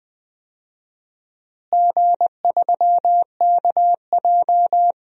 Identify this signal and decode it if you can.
Morse code, heard as a keyed tone, about 21 wpm, 700 Hz. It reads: G3KJ
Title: DXCC-100 Call Signs - 4 Characters - Repeated 2x - 20wpm